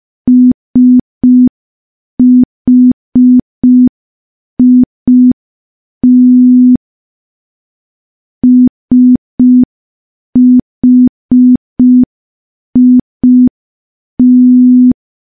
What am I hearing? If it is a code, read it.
Morse code, heard as a keyed tone, about 5 wpm, 250 Hz. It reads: SHIT SHIT
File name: morse / morse-training-05WPM